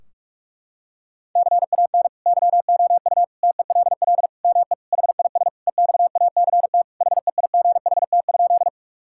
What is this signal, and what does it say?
Morse code, heard as a keyed tone, about 41 wpm, 700 Hz. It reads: CAN YOU TELL ME HIS EXACT HEIGHT?
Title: Sentences from Top 1600 Words 45wpm